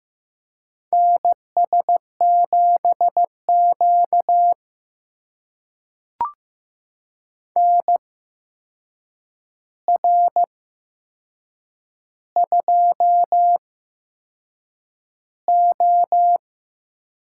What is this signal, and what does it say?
Morse code, heard as a keyed tone, about 15 wpm, spaced as for 6 wpm, 700 Hz. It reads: NS7Q NR2O
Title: USA Call Signs - 4 Characters - 8x Character Spacing - 15wpm